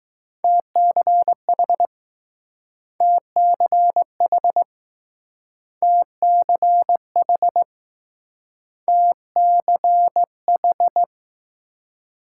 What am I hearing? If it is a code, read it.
Morse code, heard as a keyed tone, about 21 wpm, 700 Hz. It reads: TCH TCH TCH TCH